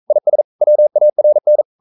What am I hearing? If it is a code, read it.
Morse code, heard as a keyed tone, about 42 wpm, 600 Hz. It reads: IS WARN
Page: Romanen Never af Ken Follet på engelsk.